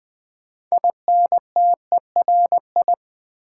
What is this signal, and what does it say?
Morse code, heard as a keyed tone, about 20 wpm, 700 Hz. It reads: INTERI
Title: Build A Word - 2 Letters At A Time - 8-Letter Words - 20wpm